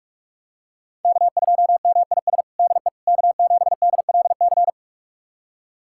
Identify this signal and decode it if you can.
Morse code, heard as a keyed tone, about 47 wpm, 700 Hz. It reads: K2MIS DE K7DLC
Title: USA Callsign DE Callsign - 5 Characters - 45wpm